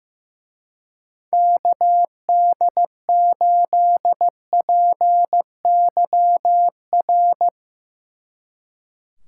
Morse code, heard as a keyed tone, about 16 wpm, 700 Hz. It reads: KD8PYR
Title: Speed Racing - USA 2x3 Call Signs 15wpm